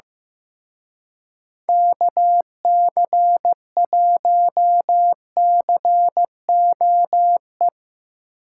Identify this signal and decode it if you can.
Morse code, heard as a keyed tone, about 17 wpm, 700 Hz. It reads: KC1COE
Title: Speed Racing - USA Call Signs 15wpm